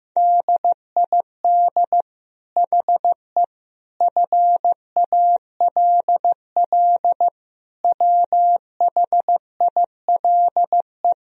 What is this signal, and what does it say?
Morse code, heard as a keyed tone, about 15 wpm, 700 Hz. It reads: DID HE FALL WHILE
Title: Sentences from Top 1000 Words - Review Part 2 - 15wpm